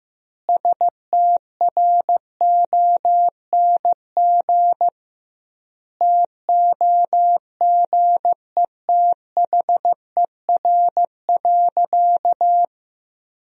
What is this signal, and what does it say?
Morse code, heard as a keyed tone, about 15 wpm, 700 Hz. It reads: STRONG TOGETHER.
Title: 2x Word-Spacing - Sentences from Top 500 Words - 15wpm